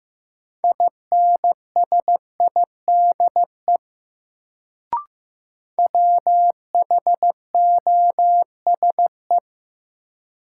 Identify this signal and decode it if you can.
Morse code, heard as a keyed tone, about 15 wpm, 700 Hz. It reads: INSIDE WHOSE
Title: Sentences from Top 1200 Words - New Word Drill Alt - 15wpm